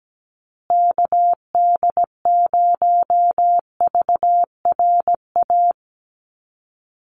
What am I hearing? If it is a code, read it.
Morse code, heard as a keyed tone, about 17 wpm, 700 Hz. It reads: KD0VRA
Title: USA Call Signs Repeated 2x 17wpm